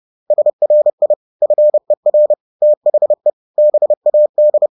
Morse code, heard as a keyed tone, about 30 wpm, 600 Hz. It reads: SRI FER THE BAD
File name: 30wpm-7